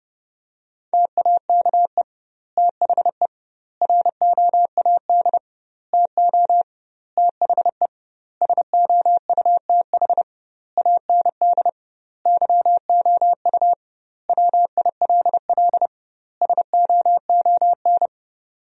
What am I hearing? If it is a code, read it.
Morse code, heard as a keyed tone, about 30 wpm, 700 Hz. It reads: TAKE THE ROAD TO THE SOUTH AND YOU WILL SOON
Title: 2x Word-Spacing - Sentences from Top 400 Words - 30wpm